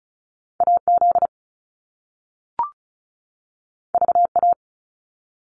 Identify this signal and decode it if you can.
Morse code, heard as a keyed tone, about 36 wpm, 700 Hz. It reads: AZ VA
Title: US States 35wpm